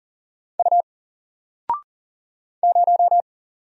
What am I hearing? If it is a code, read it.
Morse code, heard as a keyed tone, about 40 wpm, 700 Hz. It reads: U 0
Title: Single Letter-Number Alt 40wpm